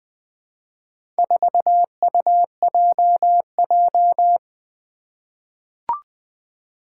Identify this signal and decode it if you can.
Morse code, heard as a keyed tone, about 21 wpm, 700 Hz. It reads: 4UJJ
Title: Four Challenging Letter-Number Combinations 20wpm